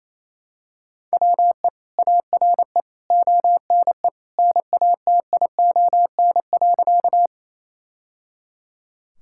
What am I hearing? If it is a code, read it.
Morse code, heard as a keyed tone, about 27 wpm, 700 Hz. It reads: WE ARE ONE NATION.
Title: Sentences from Top 1000 Words - Review Part 6 - 28wpm